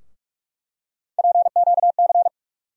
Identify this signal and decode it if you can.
Morse code, heard as a keyed tone, about 46 wpm, 700 Hz. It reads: PQC
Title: USA Call Sign Postfixes - 3 Letters 45wpm